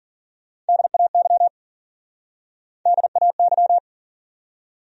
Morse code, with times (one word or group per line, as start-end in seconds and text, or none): 0.7-1.5 DAY
2.9-3.8 DAY